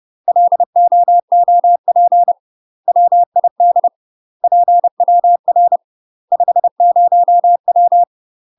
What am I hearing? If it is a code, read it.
Morse code, heard as a keyed tone, about 30 wpm, 700 Hz. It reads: LOOP WID PWR 50W
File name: UK MASTER QSOs TRAINING / 30WPM / UK MASTER QSOs 0049